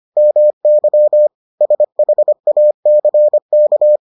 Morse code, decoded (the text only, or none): MY SHACK